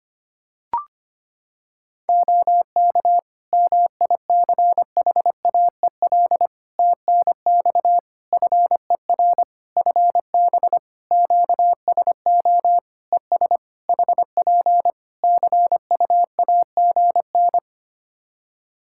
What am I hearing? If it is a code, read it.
Morse code, heard as a keyed tone, about 25 wpm, 700 Hz. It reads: OK MICHAEL TNX FER FB QSO ES HP CUAGN
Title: K1USN SST QSO Practice 25wpm